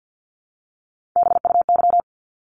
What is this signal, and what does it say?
Morse code, heard as a keyed tone, about 50 wpm, 700 Hz. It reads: BUY
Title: Top 501-750 Words 50wpm